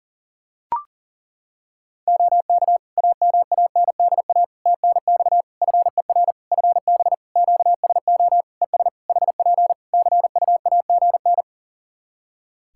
Morse code, heard as a keyed tone, about 37 wpm, 700 Hz. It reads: OK AMANDA TNX FER FB QSO ES HP CUAGN
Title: Practice QSO - Exchange 4 - Ending - 40wpm